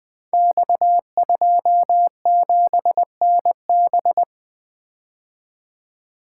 Morse code, heard as a keyed tone, about 20 wpm, 700 Hz. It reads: X27NB
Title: Code Groups - 5 Letter:Numbers - 20wpm